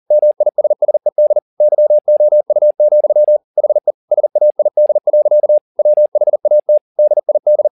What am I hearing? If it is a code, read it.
Morse code, heard as a keyed tone, about 40 wpm, 600 Hz. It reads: MISSED YOU, HE SAID. WHAT DID
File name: the old man and the sea 2 40wpm